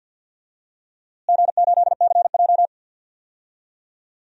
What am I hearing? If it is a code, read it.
Morse code, heard as a keyed tone, about 51 wpm, 700 Hz. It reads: G8CJ
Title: DXCC-100 Call Signs - 4 Characters - Repeated 2x - 50wpm